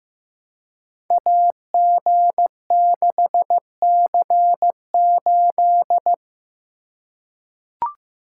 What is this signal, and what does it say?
Morse code, heard as a keyed tone, about 15 wpm, 700 Hz. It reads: AG6C8
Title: Code Groups - 5 Letter:Numbers - 15wpm